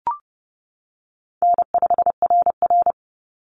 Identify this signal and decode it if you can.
Morse code, heard as a keyed tone, about 30 wpm, 700 Hz. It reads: N5RR
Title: Easy Call Signs - 30wpm